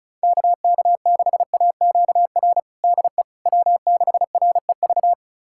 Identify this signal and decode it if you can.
Morse code, heard as a keyed tone, about 36 wpm, 700 Hz. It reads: KK6AQR DE W6REV
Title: USA Callsign DE Callsign 35wpm